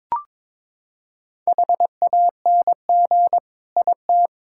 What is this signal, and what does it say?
Morse code, heard as a keyed tone, about 22 wpm, 700 Hz. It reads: HANG IT
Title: Sentences from Top 1600 Words 22wpm